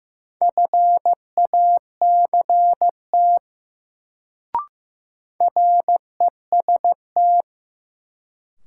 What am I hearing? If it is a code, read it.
Morse code, heard as a keyed tone, about 15 wpm, 700 Hz. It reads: FACT REST
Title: Sentences from Top 500 Words - New Word Drill - 15wpm